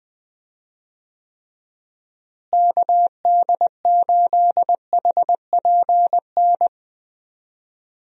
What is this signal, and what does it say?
Morse code, heard as a keyed tone, about 20 wpm, 700 Hz. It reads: KD8HPN
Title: USA Call Signs - 6 Characters - Repeated 3x - 20wpm